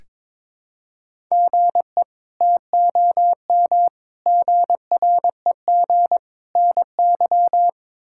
Morse code, heard as a keyed tone, about 22 wpm, 700 Hz. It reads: GE TOM GREG NY
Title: K1USN SST QSO Practice - 22wpm